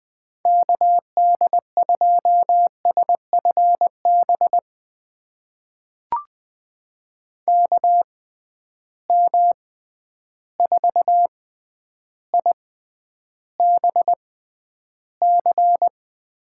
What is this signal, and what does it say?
Morse code, heard as a keyed tone, about 20 wpm, spaced as for 10 wpm, 700 Hz. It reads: KD2SFB KM4IBC
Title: USA Call Signs - 6 Characters - 6x Character Spacing - 20wpm